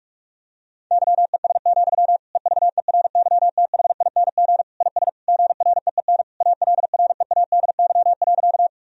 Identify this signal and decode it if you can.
Morse code, heard as a keyed tone, about 46 wpm, 700 Hz. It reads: YES, EVERYTHING IS GREEN ALREADY.
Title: Sentences from Top 1000 Words - Review Part 2 - 45wpm